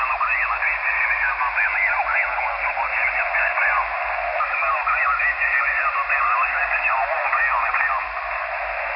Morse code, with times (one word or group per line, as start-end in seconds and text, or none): none